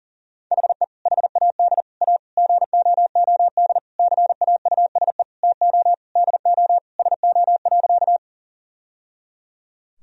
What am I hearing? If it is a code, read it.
Morse code, heard as a keyed tone, about 40 wpm, 700 Hz. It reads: HE HAD A GOOD CAUSE TO DO SO.
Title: Sentences from Top 1000 Words - Review Part 1 - 40wpm